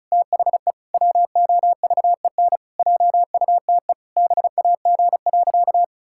Morse code, heard as a keyed tone, about 34 wpm, 700 Hz. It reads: THE WOVEN JUTE BAG.
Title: Pangram Challenge v2 - 35wpm